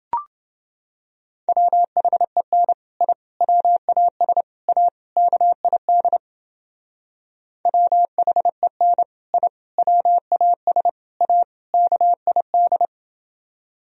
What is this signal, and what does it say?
Morse code, heard as a keyed tone, about 29 wpm, 700 Hz. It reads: WHEN I WAS A KID WHEN I WAS A KID